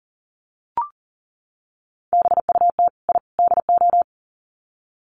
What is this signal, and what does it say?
Morse code, heard as a keyed tone, about 38 wpm, 700 Hz. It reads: BUT I DO
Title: Sets of 3 Words 40wpm